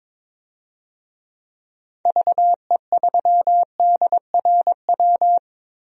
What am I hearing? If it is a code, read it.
Morse code, heard as a keyed tone, about 22 wpm, 700 Hz. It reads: VE3DRW
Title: DXCC-100 Call Signs - 6 Characters - Repeated 3x - 22wpm